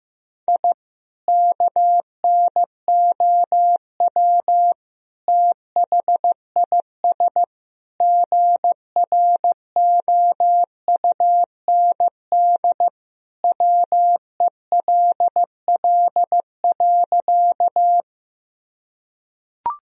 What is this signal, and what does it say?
Morse code, heard as a keyed tone, about 15 wpm, 700 Hz. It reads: I KNOW THIS GROUND WELL.
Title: Sentences from Top 1000 Words - Review Part 2 - 15wpm